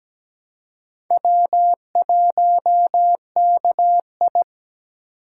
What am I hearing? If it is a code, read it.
Morse code, heard as a keyed tone, about 20 wpm, 700 Hz. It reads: W1KI
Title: Speed Racing - USA 1x2 Call Signs 17wpm